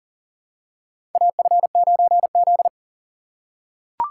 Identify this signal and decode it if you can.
Morse code, heard as a keyed tone, about 40 wpm, 700 Hz. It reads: AF9Z
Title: USA Call Signs - 4 Characters - 6x Character Spacing - 40wpm